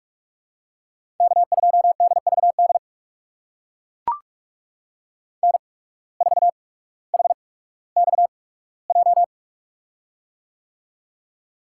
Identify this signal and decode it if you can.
Morse code, heard as a keyed tone, about 44 wpm, spaced as for 19 wpm, 700 Hz. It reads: K2DVD N4HXJ
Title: USA Call Signs - 5 Characters - 8x Character Spacing - 45wpm